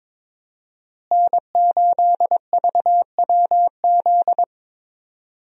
Morse code, heard as a keyed tone, about 22 wpm, 700 Hz. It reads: N8VWZ
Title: USA Call Signs - 5 Characters - Repeated 3x - 22wpm